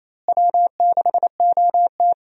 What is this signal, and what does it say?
Morse code, heard as a keyed tone, about 28 wpm, 700 Hz. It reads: W6OT
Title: USA Call Signs - 4 Characters - Repeated 3x - 28wpm